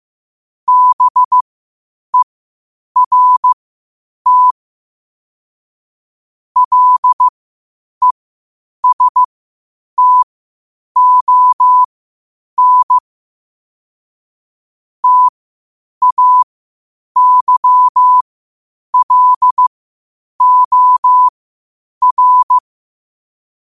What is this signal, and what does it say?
Morse code, heard as a keyed tone, about 15 wpm, spaced as for 8 wpm, 1000 Hz. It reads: BERT LESTON TAYLOR
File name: 8wpm